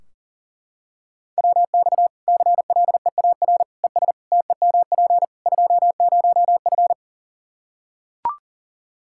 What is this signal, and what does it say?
Morse code, heard as a keyed tone, about 40 wpm, 700 Hz. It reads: WX CLEAR ES TEMP 20F
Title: K1USN SST QSO Practice 40wpm